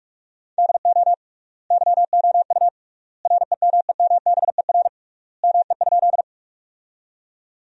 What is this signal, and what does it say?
Morse code, heard as a keyed tone, about 47 wpm, 700 Hz. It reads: DO YOU REMEMBER ME?
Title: 3x Word-Spacing - Sentences from Top 400 Words - 45wpm